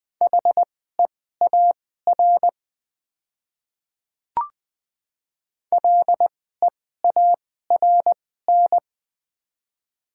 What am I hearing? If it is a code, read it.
Morse code, heard as a keyed tone, about 20 wpm, spaced as for 14 wpm, 700 Hz. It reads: HEAR LEARN
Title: 2x Character Spacing - Top 300 Words - 20wpm